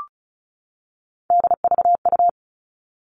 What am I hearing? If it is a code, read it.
Morse code, heard as a keyed tone, about 34 wpm, 700 Hz. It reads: DVU
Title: All Cut Numbers - 3-Digit 35wpm